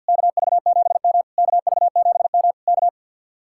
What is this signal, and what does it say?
Morse code, heard as a keyed tone, about 66 wpm, spaced as for 58 wpm, 700 Hz. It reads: KV7M KV7M K
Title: Speed-Racing Common QSO Phrases Encore - 50wpm